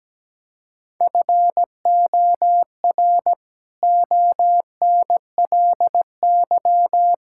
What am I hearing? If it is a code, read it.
Morse code, heard as a keyed tone, about 17 wpm, 700 Hz. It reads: FOR ONLY
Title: Pangram Challenge v2 - 17wpm